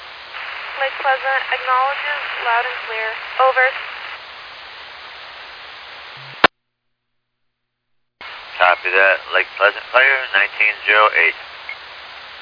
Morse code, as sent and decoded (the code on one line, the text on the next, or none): none